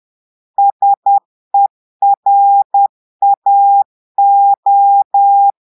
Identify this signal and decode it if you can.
Morse code, heard as a keyed tone, about 10 wpm, 800 Hz. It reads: SERAO
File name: portugais texte en clair 2 vitesse 10 mots minutes